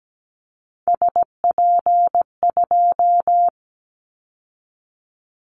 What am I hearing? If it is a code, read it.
Morse code, heard as a keyed tone, about 17 wpm, 700 Hz. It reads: SP2
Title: Callsign Prefixes for Top 25 DXCC - 17wpm